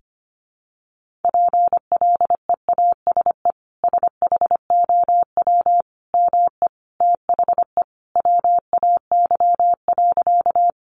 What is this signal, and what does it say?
Morse code, heard as a keyed tone, about 25 wpm, 700 Hz. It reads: PLEASE SHOW ME THE WAY.
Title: Sentences from Top 1500 Words 25wpm